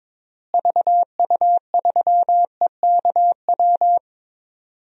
Morse code, heard as a keyed tone, about 21 wpm, 700 Hz. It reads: VU3EKW
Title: DXCC-100 Call Signs - 6 Characters - Repeated 3x - 22wpm